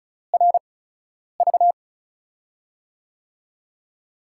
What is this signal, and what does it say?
Morse code, heard as a keyed tone, about 36 wpm, spaced as for 10 wpm, 700 Hz. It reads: RV
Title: USA Call Signs - 4 Characters - 8x Character Spacing - 35wpm